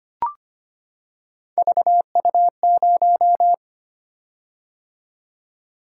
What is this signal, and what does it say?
Morse code, heard as a keyed tone, about 25 wpm, 700 Hz. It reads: VU0
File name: Three Letter-Number Combinations - 25wpm